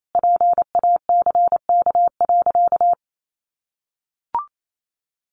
Morse code, heard as a keyed tone, about 28 wpm, 700 Hz. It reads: PACK.
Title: Sentences from Top 1300 Words 28wpm